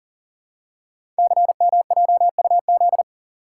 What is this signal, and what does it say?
Morse code, heard as a keyed tone, about 40 wpm, 700 Hz. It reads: CMJUZ
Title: Code Groups - 5 Letters - 40wpm